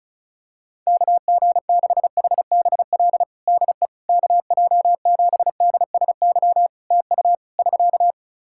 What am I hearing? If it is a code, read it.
Morse code, heard as a keyed tone, about 36 wpm, 700 Hz. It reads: KG6HBL DE KJ7DSY TU <SK>
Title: K1USN SST QSO Practice 35wpm